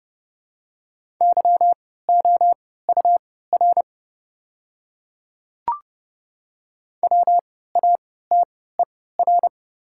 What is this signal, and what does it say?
Morse code, heard as a keyed tone, about 31 wpm, spaced as for 17 wpm, 690 Hz. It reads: YOUR WATER
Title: 3x Character Spacing - Top 200 Words - 30wpm